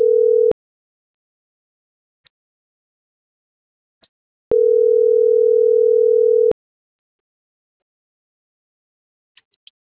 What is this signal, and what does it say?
Ringback tone (North America), 440 + 480 Hz.